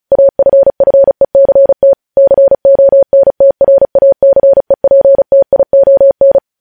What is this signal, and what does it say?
Morse code, heard as a keyed tone, about 36 wpm, 550 Hz. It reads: AFFECT CONTRACEPTION
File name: podcast / News headlines-1x-35x35